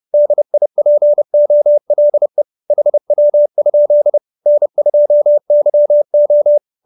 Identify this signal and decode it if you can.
Morse code, heard as a keyed tone, about 30 wpm, 590 Hz. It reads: DIPOLE HW? N2YO
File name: QSO316 30